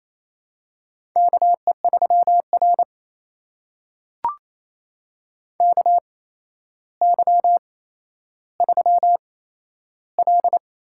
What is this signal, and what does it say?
Morse code, heard as a keyed tone, about 28 wpm, spaced as for 12 wpm, 700 Hz. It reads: KE3R KY3L
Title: USA Call Signs - 4 Characters - 8x Character Spacing - 28wpm